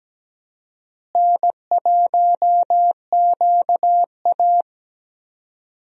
Morse code, heard as a keyed tone, about 17 wpm, 700 Hz. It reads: N1QA
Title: USA Call Signs - 4 Characters - Repeated 2x - 17wpm